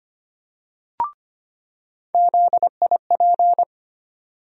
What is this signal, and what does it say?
Morse code, heard as a keyed tone, about 25 wpm, 700 Hz. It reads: ZIP
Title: USA Call Sign Postfixes - 3 Letters 25wpm